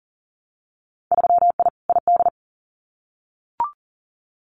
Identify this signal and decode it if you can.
Morse code, heard as a keyed tone, about 39 wpm, 700 Hz. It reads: 3I ID
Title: Winter Field Day Exchanges Encore 40wpm